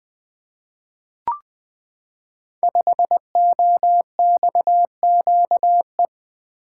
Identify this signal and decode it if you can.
Morse code, heard as a keyed tone, about 20 wpm, 700 Hz. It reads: 5OXQE
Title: Code Groups - 5 Letter:Numbers - 20wpm